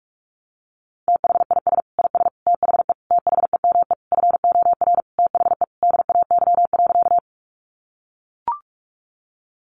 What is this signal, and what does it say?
Morse code, heard as a keyed tone, about 44 wpm, 700 Hz. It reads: THIS IS THE THEME FOR THE DAY.
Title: Sentences from Top 1100 Words 45wpm